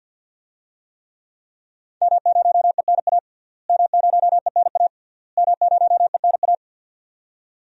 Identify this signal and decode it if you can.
Morse code, heard as a keyed tone, about 47 wpm, 700 Hz. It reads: M0ENA M0ENA M0ENA
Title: DXCC-100 Call Signs - 5 Characters - Repeated 3x - 50wpm